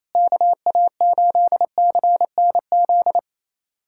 Morse code, heard as a keyed tone, about 27 wpm, 700 Hz. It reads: KA8CNZ
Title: USA Callsign DE Callsign 28wpm